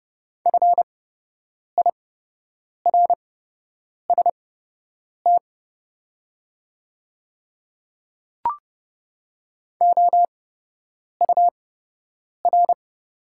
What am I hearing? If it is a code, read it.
Morse code, heard as a keyed tone, about 30 wpm, spaced as for 8 wpm, 700 Hz. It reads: FIRST OUR
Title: 8x Character Spacing - Top 200 Words - 30wpm